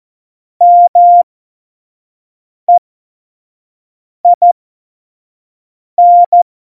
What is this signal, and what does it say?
Morse code, heard as a keyed tone, about 14 wpm, spaced as for 5 wpm, 700 Hz. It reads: MEIN
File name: Morsekurs / Witze / Witz 16. Januar 2018, 25 F70